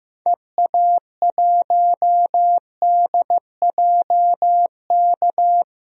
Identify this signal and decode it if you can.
Morse code, heard as a keyed tone, about 15 wpm, 700 Hz. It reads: EA1DJK
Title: DXCC-100 Call Signs - 6 Characters - 15wpm